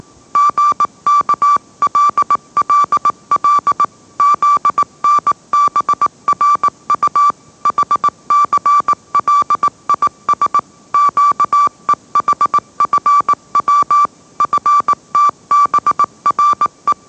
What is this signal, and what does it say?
Morse code, heard as a keyed tone, about 18 wpm, 1160 Hz. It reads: GKLLL ZNBRU HCLIS QEHFW FTBRE
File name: mrx / mrx081 20wpm